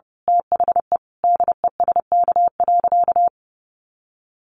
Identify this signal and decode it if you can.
Morse code, heard as a keyed tone, about 30 wpm, 700 Hz. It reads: THE DESK.
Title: A Sound Of Thunder 30wpm